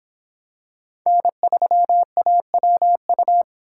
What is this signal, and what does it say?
Morse code, heard as a keyed tone, about 24 wpm, 700 Hz. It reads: N3AWU